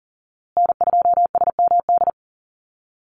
Morse code, heard as a keyed tone, about 40 wpm, 700 Hz. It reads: N2SMD
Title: USA Call Signs Repeated 2x 40wpm